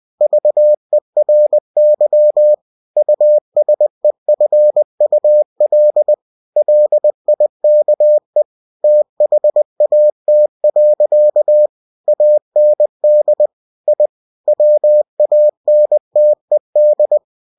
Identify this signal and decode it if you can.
Morse code, heard as a keyed tone, about 20 wpm, 600 Hz. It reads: VERY USEFUL LIKE THAT. AND I WANTED
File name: cw-books / winnie-the-pooh / winnie-20wpm-page-0051